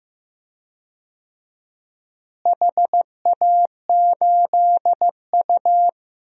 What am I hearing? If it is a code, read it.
Morse code, heard as a keyed tone, about 15 wpm, 700 Hz. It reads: HA8U
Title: DXCC-100 Call Signs - 4 Characters - Repeated 3x - 15wpm